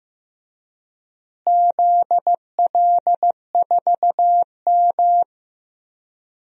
Morse code, heard as a keyed tone, about 15 wpm, 700 Hz. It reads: ZL4M
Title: DXCC-100 Call Signs - 4 Characters - Repeated 2x - 15wpm